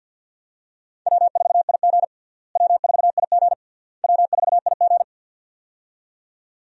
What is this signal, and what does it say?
Morse code, heard as a keyed tone, about 46 wpm, 700 Hz. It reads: W4IG W4IG W4IG
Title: USA Call Signs - 4 Characters - Repeated 3x - 50wpm